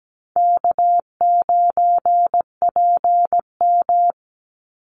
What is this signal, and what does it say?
Morse code, heard as a keyed tone, about 17 wpm, 700 Hz. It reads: K9PM
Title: NA SOTA Chaser Callsigns 17wpm